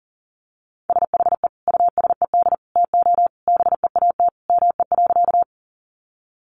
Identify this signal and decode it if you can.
Morse code, heard as a keyed tone, about 40 wpm, 700 Hz. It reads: SHE USED TO BEAT ME.
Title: Sentences from Top 1300 Words 40wpm